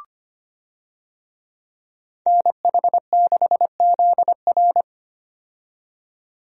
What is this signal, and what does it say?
Morse code, heard as a keyed tone, about 25 wpm, 700 Hz. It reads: NH6ZR
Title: USA Call Signs - 5 Characters - 25wpm